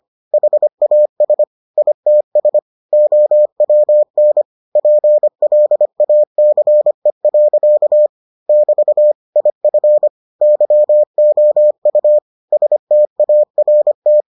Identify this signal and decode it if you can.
Morse code, heard as a keyed tone, about 25 wpm, 600 Hz. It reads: HAS ITS OWN PLACE. <BT> IF YOU START